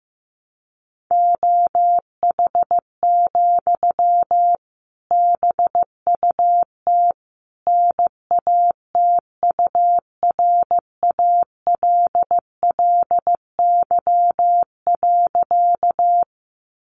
Morse code, heard as a keyed tone, about 15 wpm, 700 Hz. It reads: OH, BUT NATURALLY.
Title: Sentences from Top 1300 Words 15wpm